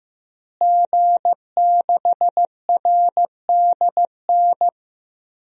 Morse code, heard as a keyed tone, about 15 wpm, 700 Hz. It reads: G6RDN
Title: DXCC-100 Call Signs - 5 Characters - Repeated 2x - 15wpm